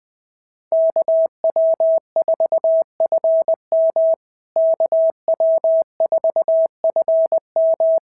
Morse code, heard as a keyed tone, about 18 wpm, 650 Hz. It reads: KW4FM KW4FM
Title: POTA QSO Practice - 20wpm